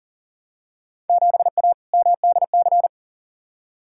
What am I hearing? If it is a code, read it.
Morse code, heard as a keyed tone, about 37 wpm, 700 Hz. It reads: 7A MDC
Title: Field Day Exchanges Encore - 40wpm